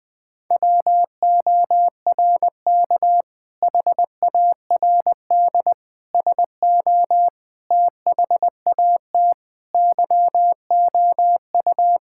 Morse code, heard as a keyed tone, about 20 wpm, 700 Hz. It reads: WORK HARD SO THAT YOU
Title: Sentences with 2-Word Phrases - 20wpm